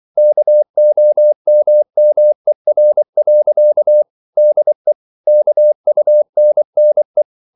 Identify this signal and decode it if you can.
Morse code, heard as a keyed tone, about 24 wpm, 600 Hz. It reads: KOMMER. DE KUNNE
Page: Ferie 24 wpm